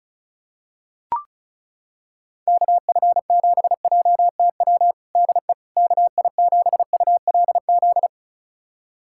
Morse code, heard as a keyed tone, about 36 wpm, 700 Hz. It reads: KF7JTW DE KI7ULZ
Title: USA Callsign DE Callsign 35wpm